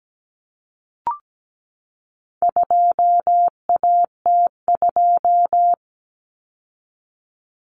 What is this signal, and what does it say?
Morse code, heard as a keyed tone, about 17 wpm, 700 Hz. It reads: 2AT2
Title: Common Cut Numbers - 4-Digit 17wpm